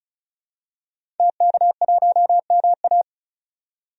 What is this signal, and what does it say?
Morse code, heard as a keyed tone, about 36 wpm, 700 Hz. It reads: TK1MA
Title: DXCC-100 Call Signs - 5 Characters - 35wpm